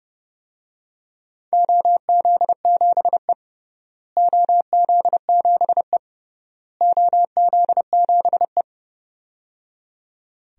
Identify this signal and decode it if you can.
Morse code, heard as a keyed tone, about 30 wpm, 700 Hz. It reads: OZ7E OZ7E OZ7E
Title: DXCC-100 Call Signs - 4 Characters - Repeated 3x - 30wpm